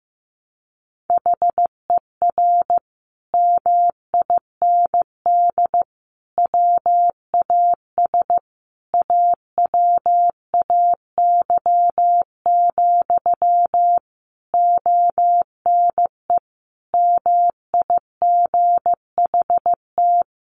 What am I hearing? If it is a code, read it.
Morse code, heard as a keyed tone, about 15 wpm, 700 Hz. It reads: HER MIND WAS AWAY, ONE MIGHT
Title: Sentences from Top 1500 Words 15wpm